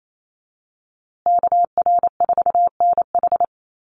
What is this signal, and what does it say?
Morse code, heard as a keyed tone, about 28 wpm, 700 Hz. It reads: KR4NH
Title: USA Call Signs Repeated 3x 28wpm